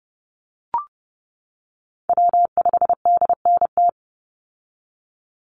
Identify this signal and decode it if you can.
Morse code, heard as a keyed tone, about 30 wpm, 700 Hz. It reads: W5DNT
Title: Call Signs - 30wpm